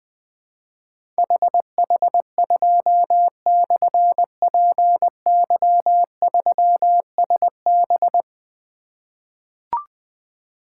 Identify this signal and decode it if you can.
Morse code, heard as a keyed tone, about 20 wpm, 700 Hz. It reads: HH2/PY3SB
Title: Difficult Call Signs 20wpm